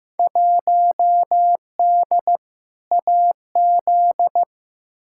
Morse code, heard as a keyed tone, about 15 wpm, 700 Hz. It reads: 1D AZ
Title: Field Day Exchanges - 15wpm